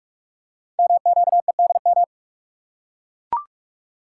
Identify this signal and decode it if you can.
Morse code, heard as a keyed tone, about 44 wpm, 700 Hz. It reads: MQEDM
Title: Code Groups - 5 Letter:Numbers - 45wpm